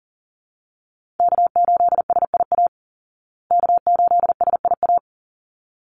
Morse code, heard as a keyed tone, about 40 wpm, 700 Hz. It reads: K8SIA K8SIA
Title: Call Signs Repeated 3x - 40wpm